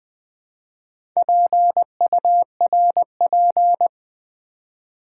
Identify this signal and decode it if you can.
Morse code, heard as a keyed tone, about 20 wpm, 690 Hz. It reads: PURP
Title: Build A Word - 2 Letters At A Time - 6-Letter Words - 20wpm